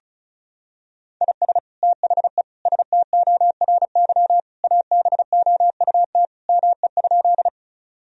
Morse code, heard as a keyed tone, about 34 wpm, 700 Hz. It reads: IS THE STORY ABOUT ME?
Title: Sentences from Top 1000 Words - Review Part 1 - 35wpm